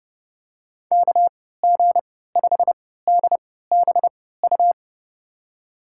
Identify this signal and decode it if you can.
Morse code, heard as a keyed tone, about 30 wpm, spaced as for 17 wpm, 700 Hz. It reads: KG5DBU
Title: USA Call Signs - 6 Characters - 3x Character Spacing - 30wpm